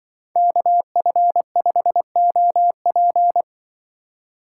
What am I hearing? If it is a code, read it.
Morse code, heard as a keyed tone, about 25 wpm, 700 Hz. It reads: KF5OP
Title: Speed Racing - USA 2x2 Call Signs 20wpm